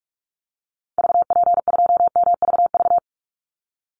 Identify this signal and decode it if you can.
Morse code, heard as a keyed tone, about 44 wpm, 700 Hz. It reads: VP2MVV
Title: Call Signs Repeated 2x - 45wpm